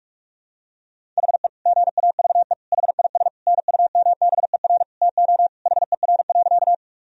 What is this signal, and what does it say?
Morse code, heard as a keyed tone, about 46 wpm, 700 Hz. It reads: HE GAVE HIS NUMBER TO HER.
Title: Sentences from Top 1000 Words - Review Part 1 - 45wpm